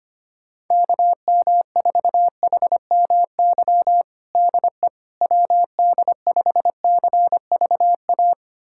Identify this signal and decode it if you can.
Morse code, heard as a keyed tone, about 25 wpm, 700 Hz. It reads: KM4HMY DE WD5CVA
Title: USA Callsign DE Callsign - 6 Characters - 25wpm